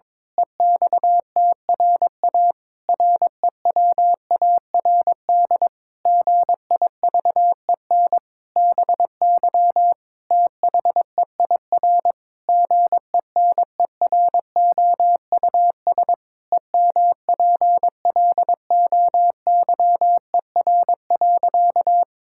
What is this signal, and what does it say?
Morse code, heard as a keyed tone, about 21 wpm, 700 Hz. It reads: EXTRA REWARD GIVEN BY THEIR GENEROUS EMPLOYER.
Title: Pangram Challenge v2 - 22wpm